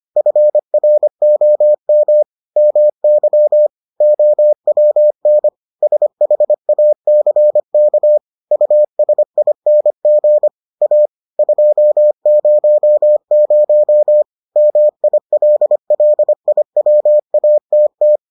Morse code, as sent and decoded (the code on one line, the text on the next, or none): ..-. .-. --- -- / -- -.-- / --- .-- -. / ... .... .- -.-. -.- / ..- ... .. -. --. / .- / ..--- ----- ----- / -- .. .-.. .-.. .. .-- .- - -
FROM MY OWN SHACK USING A 200 MILLIWATT